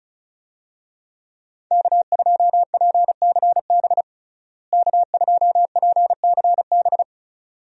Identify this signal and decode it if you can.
Morse code, heard as a keyed tone, about 36 wpm, 700 Hz. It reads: K2PCB K2PCB
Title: USA Call Signs - 5 Characters - Repeated 2x - 35wpm